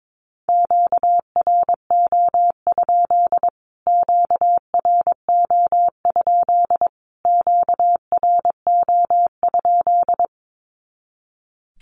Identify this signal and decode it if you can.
Morse code, heard as a keyed tone, about 22 wpm, 700 Hz. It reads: QRO? QRO? QRO?
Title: Q-Codes Repeated 3x 22wpm